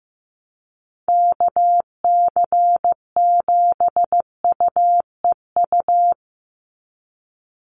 Morse code, heard as a keyed tone, about 15 wpm, 700 Hz. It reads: KC7UEU
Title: USA Call Signs 15wpm